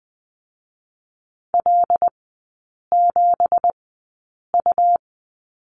Morse code, heard as a keyed tone, about 20 wpm, spaced as for 8 wpm, 700 Hz. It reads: L7U
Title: 166 Lesson 27e - Number 7 - Three Characters 20wpm